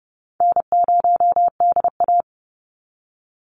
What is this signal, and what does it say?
Morse code, heard as a keyed tone, about 30 wpm, 700 Hz. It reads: N0DA
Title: NA SOTA Chaser Callsigns 30wpm